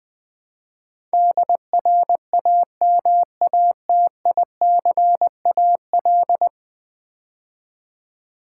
Morse code, heard as a keyed tone, about 20 wpm, 700 Hz. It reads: DRAMATICAL
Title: Build A Word - 5 Letters At A Time - 12-Letter Words - 20wpm